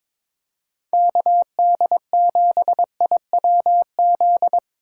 Morse code, heard as a keyed tone, about 21 wpm, 700 Hz. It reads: KD7IWZ
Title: USA Call Signs - 6 Characters - 4x Character Spacing - 22wpm